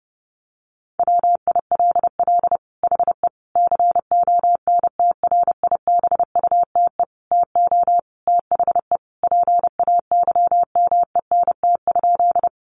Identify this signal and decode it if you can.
Morse code, heard as a keyed tone, about 28 wpm, 700 Hz. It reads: WILL HE CONTRIBUTE TO THE PAYMENT?
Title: Sentences from Top 1400 Words 30wpm